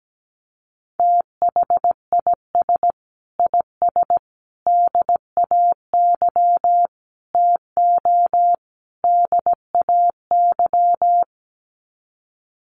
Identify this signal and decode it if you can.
Morse code, heard as a keyed tone, about 17 wpm, 700 Hz. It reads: THIS IS DAY TO DAY
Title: Sets of 5 Words Encore 17wpm